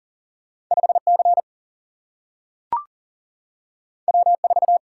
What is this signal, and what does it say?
Morse code, heard as a keyed tone, about 40 wpm, 700 Hz. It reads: 5C W4
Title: Single-Letter Single-Number 40wpm